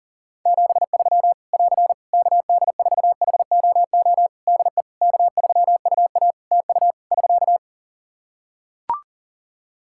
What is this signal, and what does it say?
Morse code, heard as a keyed tone, about 40 wpm, 700 Hz. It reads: 73 <AR> KD4HOO DE K3UA TU <SK>
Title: Practice QSO - Exchange 4 - Ending - 40wpm